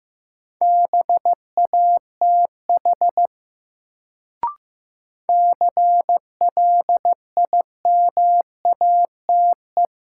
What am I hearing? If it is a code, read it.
Morse code, heard as a keyed tone, about 15 wpm, 700 Hz. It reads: BATH CLIMATE
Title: Sentences from Top 1600 Words - New Word Drill - 15wpm